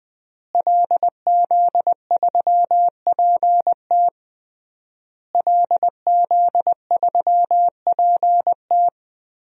Morse code, heard as a keyed tone, about 20 wpm, 700 Hz. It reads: LZ3PT LZ3PT
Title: DXCC-100 Call Signs - 5 Characters - Repeated 3x - 20wpm